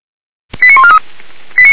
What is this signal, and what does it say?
Five-tone selective call (ZVEI), heard as quick runs of five tones, tones of about 70 ms. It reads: 89143
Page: /SDR/ZVEI/